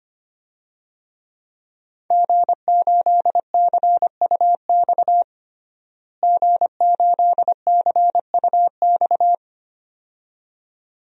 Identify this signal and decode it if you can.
Morse code, heard as a keyed tone, about 25 wpm, 700 Hz. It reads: G8CUX G8CUX
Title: DXCC-100 Call Signs - 5 Characters - Repeated 2x - 25wpm